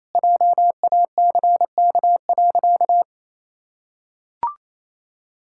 Morse code, heard as a keyed tone, about 28 wpm, 700 Hz. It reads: JACK.
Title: Pangram Challenge v2 - 28wpm